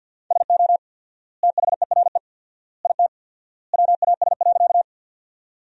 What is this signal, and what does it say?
Morse code, heard as a keyed tone, about 50 wpm, 700 Hz. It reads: SO THERE IT WAS.
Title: 4x Word-Spacing - Sentences from Top 100 Words - 50wpm